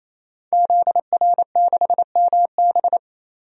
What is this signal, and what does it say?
Morse code, heard as a keyed tone, about 27 wpm, 700 Hz. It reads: ZR6MB
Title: DXCC-100 Call Signs - 5 Characters - 28wpm